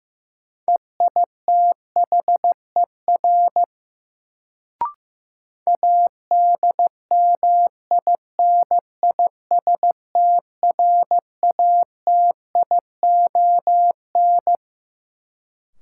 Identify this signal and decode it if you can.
Morse code, heard as a keyed tone, about 15 wpm, 700 Hz. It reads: EITHER ADMINISTRATION
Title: Sets of 5 Words - New Word Drill - 15wpm